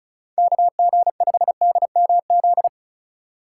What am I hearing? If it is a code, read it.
Morse code, heard as a keyed tone, about 34 wpm, 700 Hz. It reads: KG5DMZ
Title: USA Call Signs - 6 Characters - 4x Character Spacing - 35wpm